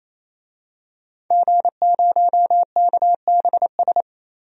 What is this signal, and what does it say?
Morse code, heard as a keyed tone, about 28 wpm, 700 Hz. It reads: G0KBS
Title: DXCC-100 Call Signs - 5 Characters - 28wpm